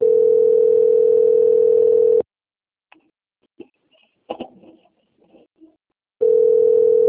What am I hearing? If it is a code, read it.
Ringback tone (North America), 440 + 480 Hz.
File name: out- - - - - .37998